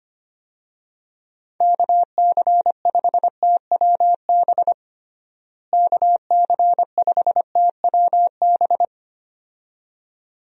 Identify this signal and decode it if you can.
Morse code, heard as a keyed tone, about 25 wpm, 700 Hz. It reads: KC5TWB KC5TWB
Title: USA Call Signs - 6 Characters - Repeated 2x - 25wpm